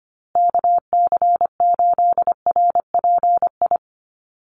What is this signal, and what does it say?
Morse code, heard as a keyed tone, about 25 wpm, 700 Hz. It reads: KC8RPI
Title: USA Call Signs Repeated 2x 25wpm